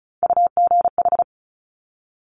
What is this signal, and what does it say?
Morse code, heard as a keyed tone, about 36 wpm, 690 Hz. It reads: UGH
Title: Top Three Letter Combinations in Words 35wpm